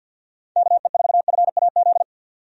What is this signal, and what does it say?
Morse code, heard as a keyed tone, about 51 wpm, 700 Hz. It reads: KE4FAZ
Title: USA Call Signs - 6 Characters - 50wpm